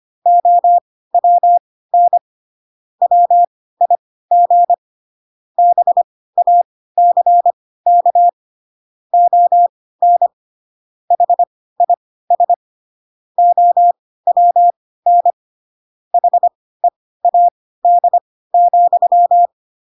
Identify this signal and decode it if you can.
Morse code, heard as a keyed tone, about 25 wpm, spaced as for 16 wpm, 700 Hz. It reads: OWN WIG BACK ON HIS OWN HEAD,
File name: Pinocchio / Pinocchio ch2 10 25 ik1hge